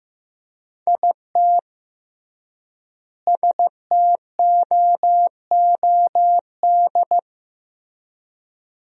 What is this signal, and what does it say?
Morse code, heard as a keyed tone, about 15 wpm, 700 Hz. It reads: IT STOOD
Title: 3x Word-Spacing - Sentences from Top 500 Words - 15wpm